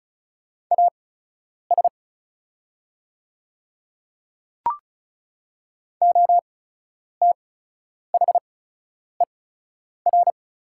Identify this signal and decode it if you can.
Morse code, heard as a keyed tone, about 35 wpm, spaced as for 10 wpm, 690 Hz. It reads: AS OTHER
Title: 8x Character Spacing - Top 100 Words - 35wpm